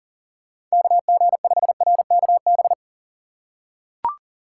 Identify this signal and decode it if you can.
Morse code, heard as a keyed tone, about 40 wpm, 700 Hz. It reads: KG5RKB
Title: USA Call Signs - 6 Characters - 4x Character Spacing - 40wpm